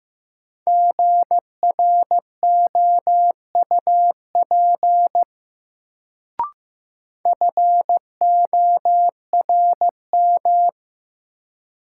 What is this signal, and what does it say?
Morse code, heard as a keyed tone, about 16 wpm, 700 Hz. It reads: GROUP FORM
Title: Sentences from Top 1000 Words - Review Part 1 - New Word Drill - 15wpm